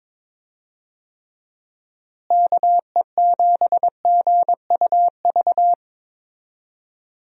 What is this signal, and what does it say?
Morse code, heard as a keyed tone, about 22 wpm, 700 Hz. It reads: KE7GUV
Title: USA Call Signs - 6 Characters - Repeated 2x - 22wpm